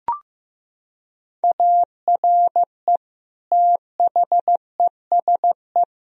Morse code, heard as a keyed tone, about 15 wpm, 700 Hz. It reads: ARE THESE
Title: Sentences from Top 1000 Words - Review Part 1 - 15wpm